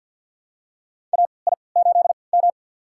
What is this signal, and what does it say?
Morse code, heard as a keyed tone, about 52 wpm, spaced as for 29 wpm, 700 Hz. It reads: AI8M
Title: USA Call Signs - 4 Characters - 3x Character Spacing - 50wpm